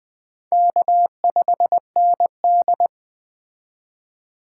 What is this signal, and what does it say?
Morse code, heard as a keyed tone, about 20 wpm, 700 Hz. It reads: K5ND
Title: Speed Racing - USA Call Signs 17wpm